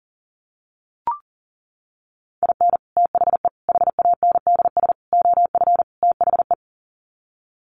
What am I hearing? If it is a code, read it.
Morse code, heard as a keyed tone, about 40 wpm, 700 Hz. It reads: IN THE HANDS OF THE
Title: Sets of 5 Words 40wpm